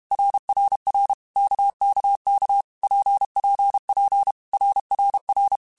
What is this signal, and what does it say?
Morse code, heard as a keyed tone, about 32 wpm, 800 Hz. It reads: RRR KKK PPP RRR
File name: Basic Letter Groups By7Groups / RKPX / RKPX@30